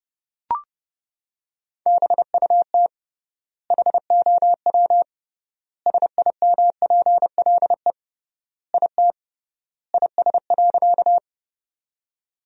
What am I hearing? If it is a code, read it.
Morse code, heard as a keyed tone, about 30 wpm, 700 Hz. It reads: BUT HOW SIMPLE IT IS.
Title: 3x Word-Spacing - Sentences from Top 400 Words - 30wpm